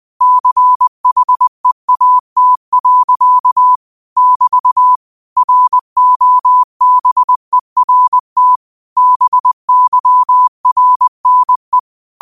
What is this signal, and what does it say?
Morse code, heard as a keyed tone, about 20 wpm, 1000 Hz. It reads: CHEAT. <BT> ROBERT BYRNE
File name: 20wpm